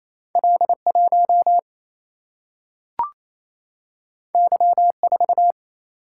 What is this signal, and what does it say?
Morse code, heard as a keyed tone, about 28 wpm, 700 Hz. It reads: L1 Y4
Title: Single-Letter Single-Number 28wpm